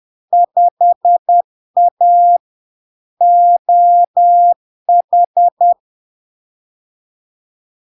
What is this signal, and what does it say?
Morse code, heard as a keyed tone, about 10 wpm, 700 Hz. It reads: 5A OH
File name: LARK-FD-2022-Log-in-CW-10wpm-700-Hz